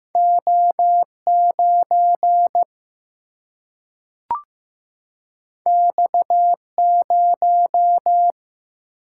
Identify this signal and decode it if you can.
Morse code, heard as a keyed tone, about 15 wpm, 700 Hz. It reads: O 9 X 0
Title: Single-Letter Single-Number 15wpm